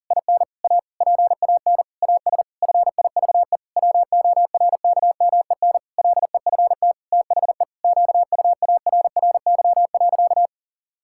Morse code, heard as a keyed tone, about 40 wpm, 700 Hz. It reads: IN A PAN AS FIVE WORKMEN LEFT THE QUARRY.
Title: Pangram Challenge v2 - 40wpm